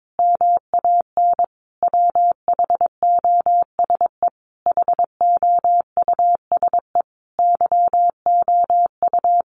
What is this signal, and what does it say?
Morse code, heard as a keyed tone, about 22 wpm, 700 Hz. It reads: MAN WHOSE HOUSE YOU
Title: Sentences from Top 1200 Words 22wpm